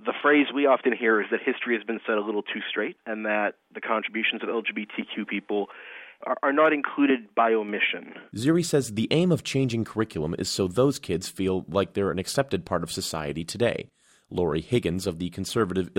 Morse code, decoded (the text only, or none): none